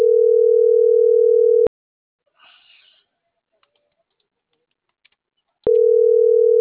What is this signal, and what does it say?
Ringback tone (North America), 440 + 480 Hz.